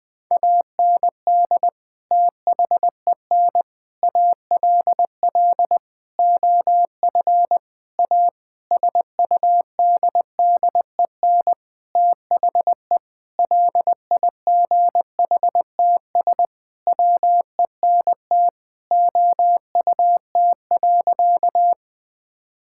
Morse code, heard as a keyed tone, about 20 wpm, 700 Hz. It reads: AND THEN ALL OF A SUDDEN THE LIGHTS WENT OUT.
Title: Sentences with 6-Word Phrases - Spoken Beginning - 20wpm